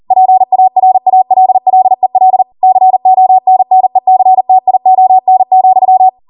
Morse code, heard as a keyed tone, about 40 wpm, 750 Hz. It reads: PARALLEL CONNECTION,